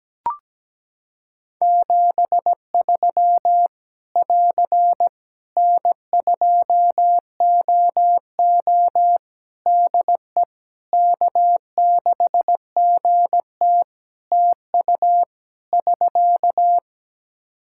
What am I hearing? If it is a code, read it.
Morse code, heard as a keyed tone, about 17 wpm, 700 Hz. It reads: 73 <AR> N2OO DE K6GT TU <SK>
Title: Practice QSO - Exchange 4 - Ending - 17wpm